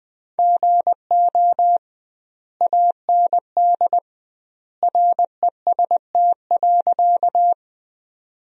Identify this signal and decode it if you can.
Morse code, heard as a keyed tone, about 20 wpm, 700 Hz. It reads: GO AND REST.
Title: 2x Word-Spacing - Sentences from Top 500 Words - 20wpm